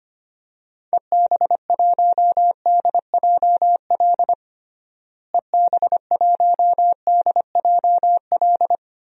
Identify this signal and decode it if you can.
Morse code, heard as a keyed tone, about 25 wpm, 700 Hz. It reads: EB1DJL EB1DJL
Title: DXCC-100 Call Signs - 6 Characters - Repeated 2x - 25wpm